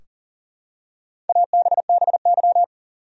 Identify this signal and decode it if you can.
Morse code, heard as a keyed tone, about 40 wpm, 700 Hz. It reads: ABBY
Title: USA Common Names - Max 4 Letters - 40wpm